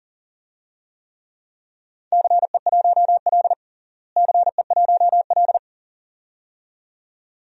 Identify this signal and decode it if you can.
Morse code, heard as a keyed tone, about 37 wpm, 700 Hz. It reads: CE1L CE1L
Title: DXCC-100 Call Signs - 4 Characters - Repeated 2x - 40wpm